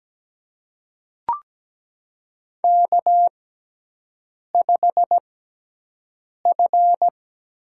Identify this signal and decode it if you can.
Morse code, heard as a keyed tone, about 17 wpm, spaced as for 6 wpm, 700 Hz. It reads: K5F
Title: USA Call Signs - 4 Characters - 6x Character Spacing - 17wpm